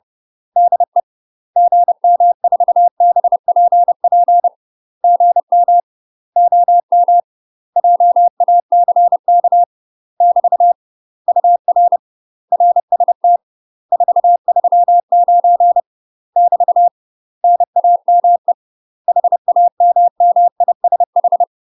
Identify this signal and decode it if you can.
Morse code, heard as a keyed tone, about 30 wpm, 700 Hz. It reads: DE GM4BPP GM OM JACK <BT> UR RST 439 <BT> NAME HAMMISH
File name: UK MASTER QSOs TRAINING / 30WPM / UK MASTER QSOs 0011